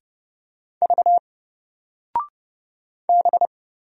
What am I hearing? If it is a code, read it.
Morse code, heard as a keyed tone, about 30 wpm, 700 Hz. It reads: V B
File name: Single-Letter-Number-30wpm